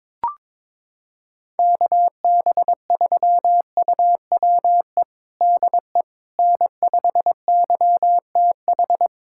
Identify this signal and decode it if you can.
Morse code, heard as a keyed tone, about 22 wpm, 700 Hz. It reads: KB3UWE DE N5YTH
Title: USA Callsign DE Callsign 22wpm